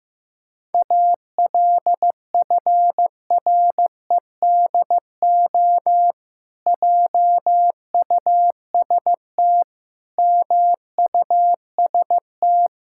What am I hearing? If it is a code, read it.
Morse code, heard as a keyed tone, about 16 wpm, 700 Hz. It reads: ALFREDO JUST MUST
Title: Pangram Challenge v2 - 15wpm